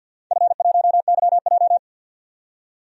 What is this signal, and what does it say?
Morse code, heard as a keyed tone, about 51 wpm, 700 Hz. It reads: F1YJ
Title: DXCC-100 Call Signs - 4 Characters - 50wpm